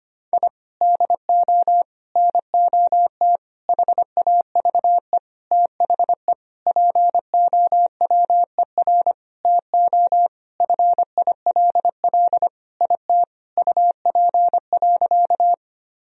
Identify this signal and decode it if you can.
Morse code, heard as a keyed tone, about 25 wpm, 700 Hz. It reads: I DO NOT HAVE THE POWER TO FILL IT UP.
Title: Sentences from Top 1000 Words - Review Part 3 - 25wpm